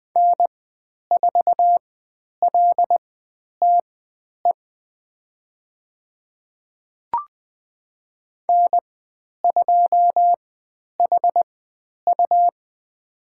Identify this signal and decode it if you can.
Morse code, heard as a keyed tone, about 20 wpm, spaced as for 10 wpm, 700 Hz. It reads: N4LTE N2HU
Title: 102 Lesson 17g - Letter u - Call Signs 20wpm